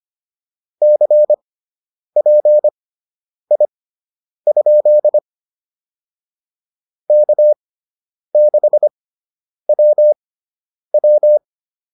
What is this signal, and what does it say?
Morse code, heard as a keyed tone, about 25 wpm, spaced as for 9 wpm, 600 Hz. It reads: CPI? K6WW